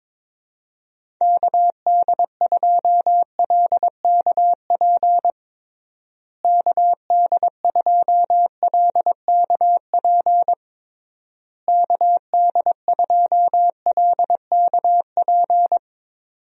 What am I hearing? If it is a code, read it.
Morse code, heard as a keyed tone, about 22 wpm, 700 Hz. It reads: KD2LKP KD2LKP KD2LKP
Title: USA Call Signs - 6 Characters - Repeated 3x - 22wpm